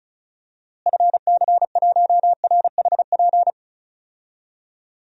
Morse code, heard as a keyed tone, about 36 wpm, 700 Hz. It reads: FC1RHP
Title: DXCC-100 Call Signs - 6 Characters - Repeated 3x - 35wpm